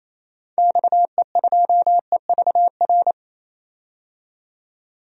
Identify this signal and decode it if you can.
Morse code, heard as a keyed tone, about 28 wpm, 700 Hz. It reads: XE2EVR
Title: DXCC-100 Call Signs - 6 Characters - 28wpm